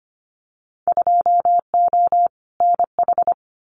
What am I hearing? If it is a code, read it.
Morse code, heard as a keyed tone, about 25 wpm, 700 Hz. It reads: 2O NH
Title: Winter Field Day Exchanges Encore 25wpm